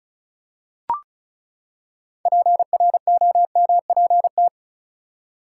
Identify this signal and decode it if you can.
Morse code, heard as a keyed tone, about 36 wpm, 700 Hz. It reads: PROMPT
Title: Six Letter Words - 35wpm